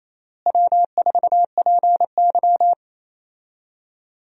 Morse code, heard as a keyed tone, about 28 wpm, 700 Hz. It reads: W4PY
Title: USA Call Signs - 4 Characters - Repeated 3x - 28wpm